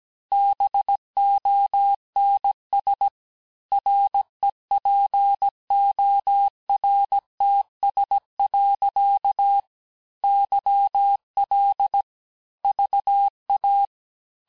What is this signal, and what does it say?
Morse code, heard as a keyed tone, about 17 wpm, 800 Hz. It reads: BONS REPORTS. YL VA
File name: cours-du-lundi files / cours-1587-FIN-de-QSO-vitesse-16MPM